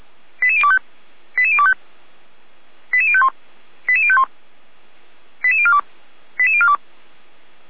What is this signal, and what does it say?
Five-tone selective call (ZVEI), heard as quick runs of five tones, tones of about 70 ms. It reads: 88025 88025 88051 88051 88042 88042
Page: SDR/ZVEI